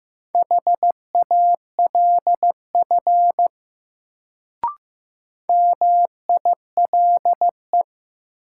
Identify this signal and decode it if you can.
Morse code, heard as a keyed tone, about 15 wpm, 700 Hz. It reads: HALF MILE
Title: Sentences from Top 400 Words - New Word Drill - 15wpm